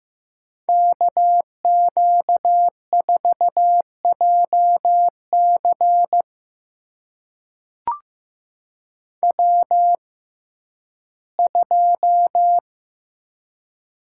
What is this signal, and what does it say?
Morse code, heard as a keyed tone, about 15 wpm, spaced as for 11 wpm, 700 Hz. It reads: KQ4JC W2
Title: USA Call Signs - 5 Characters - 6x Character Spacing - 15wpm